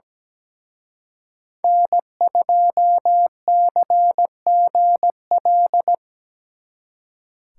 Morse code, heard as a keyed tone, about 17 wpm, 700 Hz. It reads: N2CGL
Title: Speed Racing - USA 1x3 Call Signs 17wpm